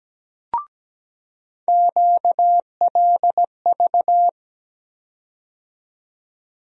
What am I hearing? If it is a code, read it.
Morse code, heard as a keyed tone, about 17 wpm, 700 Hz. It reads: QLV
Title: USA Call Sign Postfixes - 3 Letters 17wpm